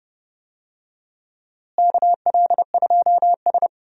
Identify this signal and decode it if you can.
Morse code, heard as a keyed tone, about 30 wpm, 700 Hz. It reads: KL2S
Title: USA Call Signs - 4 Characters - Repeated 2x - 30wpm